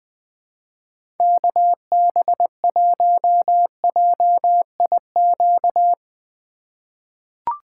Morse code, heard as a keyed tone, about 20 wpm, 700 Hz. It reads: KB1JIQ
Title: USA Call Signs - 6 Characters - 2x Character Spacing - 20wpm